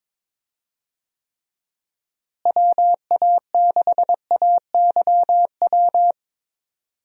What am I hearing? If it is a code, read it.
Morse code, heard as a keyed tone, about 21 wpm, 700 Hz. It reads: WA6AYW
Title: USA Call Signs - 6 Characters - Repeated 2x - 22wpm